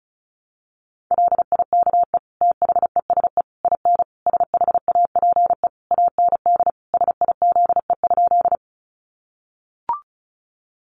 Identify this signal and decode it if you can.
Morse code, heard as a keyed tone, about 34 wpm, 700 Hz. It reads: LIKE THESE IN SHAPE AND SIZE?
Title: Sentences from Top 500 Words 35wpm